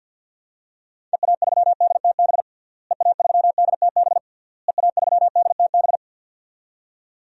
Morse code, heard as a keyed tone, about 46 wpm, 700 Hz. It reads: EA3DTB EA3DTB EA3DTB
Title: DXCC-100 Call Signs - 6 Characters - Repeated 3x - 50wpm